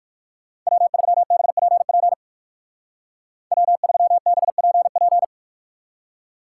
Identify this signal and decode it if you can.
Morse code, heard as a keyed tone, about 59 wpm, 700 Hz. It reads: W3BPP W3BPP